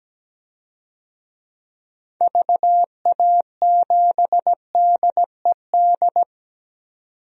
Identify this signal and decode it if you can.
Morse code, heard as a keyed tone, about 17 wpm, 700 Hz. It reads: VA7DED
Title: DXCC-100 Call Signs - 6 Characters - 17wpm